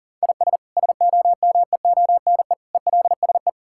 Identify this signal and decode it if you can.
Morse code, heard as a keyed tone, about 40 wpm, 700 Hz. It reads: IS SOMEONE ELSE
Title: Sentences from Top 1000 Words - Review Part 5 - 40wpm